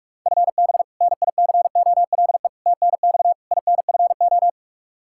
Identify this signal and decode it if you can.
Morse code, heard as a keyed tone, about 41 wpm, 700 Hz. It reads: FB NICOLE TNX INFO
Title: Practice QSO - Exchange 3 - More Info - 45wpm